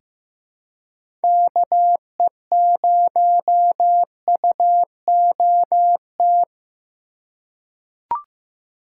Morse code, heard as a keyed tone, about 15 wpm, 700 Hz. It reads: KE0UOT
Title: USA Call Signs - 6 Characters - 4x Character Spacing - 15wpm